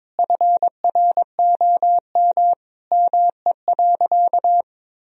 Morse code, heard as a keyed tone, about 22 wpm, 700 Hz. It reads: FROM ME.
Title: Sentences from Top 1000 Words - Review Part 2 - 22wpm